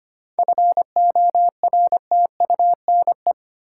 Morse code, heard as a keyed tone, about 25 wpm, 700 Hz. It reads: FORTUNE
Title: Seven Letter Words - 25wpm